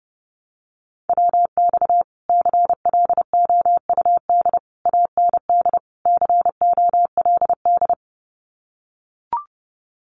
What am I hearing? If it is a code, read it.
Morse code, heard as a keyed tone, about 29 wpm, 700 Hz. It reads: WX CLOUD AND COLD
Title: Common QSO Phrases Encore 30wpm